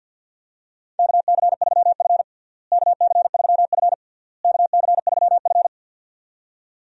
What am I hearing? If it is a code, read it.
Morse code, heard as a keyed tone, about 48 wpm, 700 Hz. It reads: KC3F KC3F KC3F
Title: USA Call Signs - 4 Characters - Repeated 3x - 50wpm